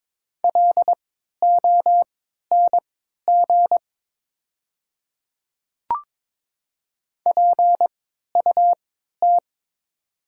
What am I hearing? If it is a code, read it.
Morse code, heard as a keyed tone, about 22 wpm, spaced as for 12 wpm, 700 Hz. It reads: LONG PUT
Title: 3x Character Spacing - Top 300 Words - 22wpm